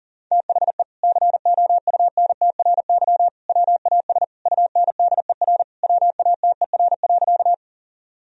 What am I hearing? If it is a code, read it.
Morse code, heard as a keyed tone, about 37 wpm, 700 Hz. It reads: THE COUNTRY WAS UNDER WATER.
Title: Sentences from Top 1000 Words - Review Part 1 - 40wpm